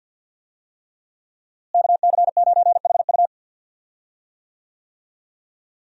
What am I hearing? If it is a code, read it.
Morse code, heard as a keyed tone, about 52 wpm, 690 Hz. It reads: KC9HU
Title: USA Call Signs - 5 Characters - Repeated 3x - 50wpm